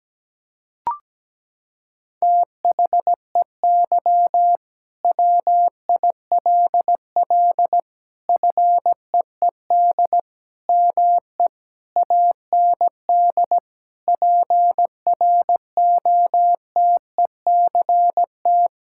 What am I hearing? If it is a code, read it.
Morse code, heard as a keyed tone, about 17 wpm, 700 Hz. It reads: THEY WILL FEED ME AND PROTECT
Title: Sentences from Top 1600 Words 17wpm